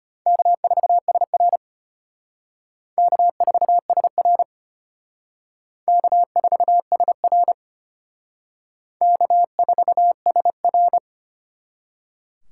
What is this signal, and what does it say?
Morse code, heard as a keyed tone, about 32 wpm, 700 Hz. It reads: K4SR K4SR K4SR K4SR